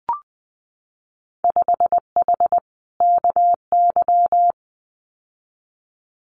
Morse code, heard as a keyed tone, about 19 wpm, 700 Hz. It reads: 5H KY
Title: Winter Field Day Exchanges Encore 20wpm